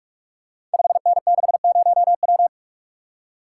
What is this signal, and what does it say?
Morse code, heard as a keyed tone, about 44 wpm, 700 Hz. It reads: 5N60W
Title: Code Groups - 5 Letter:Numbers - 45wpm